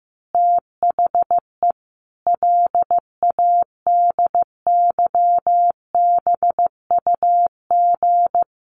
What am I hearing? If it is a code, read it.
Morse code, heard as a keyed tone, about 15 wpm, 700 Hz. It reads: THE LADYBUG
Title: Additional Easy Phrases 15wpm